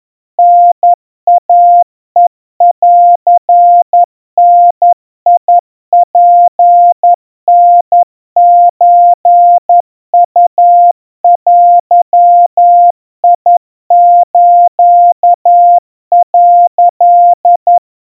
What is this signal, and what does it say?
Morse code (Japanese Wabun code), heard as a keyed tone, about 11 wpm, 700 Hz. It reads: タイヘンダツタソウデス」
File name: kawa / Morse sample / kamyu 40 cpm 700 Hz Crispy